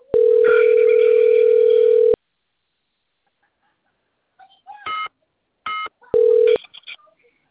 Ringback tone (North America), 440 + 480 Hz.